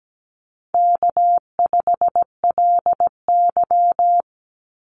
Morse code, heard as a keyed tone, about 17 wpm, 700 Hz. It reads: K5LY
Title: Easy Call Signs Repeated 2x - 17wpm